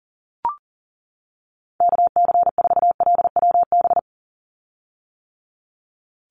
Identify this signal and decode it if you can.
Morse code, heard as a keyed tone, about 40 wpm, 700 Hz. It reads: KC4LWB
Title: USA Call Signs 40wpm